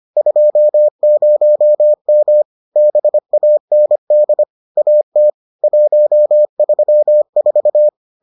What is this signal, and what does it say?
Morse code, heard as a keyed tone, about 25 wpm, 600 Hz. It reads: 20M BAND AT 134